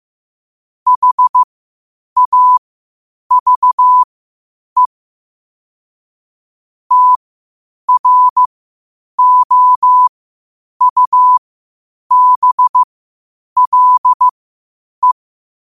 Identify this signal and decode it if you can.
Morse code, heard as a keyed tone, about 15 wpm, spaced as for 8 wpm, 1000 Hz. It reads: HAVE TROUBLE
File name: qotdfiles / 8wpm